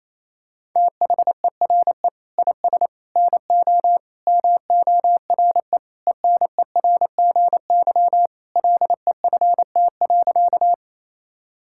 Morse code, heard as a keyed tone, about 28 wpm, 700 Hz. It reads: THERE IS NO MORE ENERGY LEFT.
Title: Sentences from Top 1000 Words - Review Part 6 - 28wpm